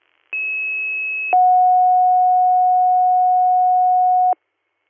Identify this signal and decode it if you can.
Two-tone page, heard as a one-second tone, then a three-second tone, about 2600 then 730 Hz. It is heard from 0.3 to 4.3 s.